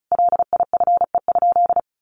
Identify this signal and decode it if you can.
Morse code, heard as a keyed tone, about 34 wpm, 700 Hz. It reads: LIFE?
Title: Sentences from Top 900 Words 35wpm